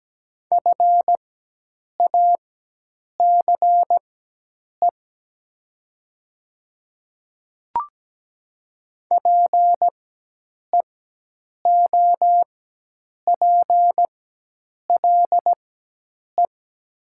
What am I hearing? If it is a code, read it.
Morse code, heard as a keyed tone, about 17 wpm, spaced as for 8 wpm, 700 Hz. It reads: FACE PEOPLE
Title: 4x Character Spacing - Top 300 Words - 17wpm